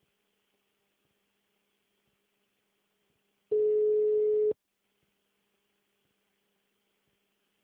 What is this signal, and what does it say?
Ringback tone (Europe, 425 Hz).